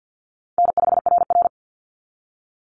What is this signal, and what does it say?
Morse code, heard as a keyed tone, about 48 wpm, 700 Hz. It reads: N5RR
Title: Call Signs Repeated 2x - 50wpm